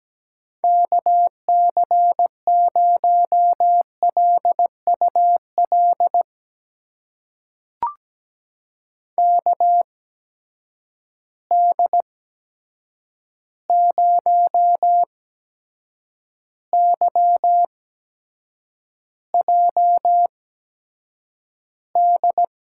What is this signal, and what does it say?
Morse code, heard as a keyed tone, about 17 wpm, spaced as for 7 wpm, 700 Hz. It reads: KC0LUL KD0YJD
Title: USA Call Signs - 6 Characters - 8x Character Spacing - 17wpm